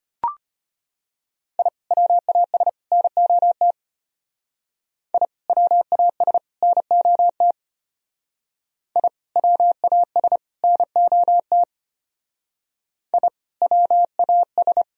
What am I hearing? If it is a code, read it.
Morse code, heard as a keyed tone, about 29 wpm, 700 Hz. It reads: I WAS NOT I WAS NOT I WAS NOT I WAS
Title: Speed-Racing - Sets of 3 Words 25wpm